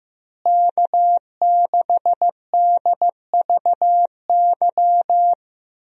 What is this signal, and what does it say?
Morse code, heard as a keyed tone, about 15 wpm, 700 Hz. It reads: K6DVY
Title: USA Call Signs - 5 Characters - Repeated 3x - 15wpm